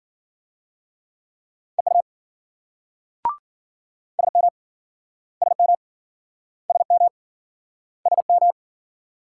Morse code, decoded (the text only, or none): EU SM SM SM SM